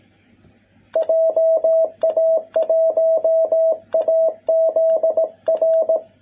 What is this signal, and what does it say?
Morse code, heard as a keyed tone, about 18 wpm, 660 Hz. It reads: JA1AZR
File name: jh8cbh / 2021cw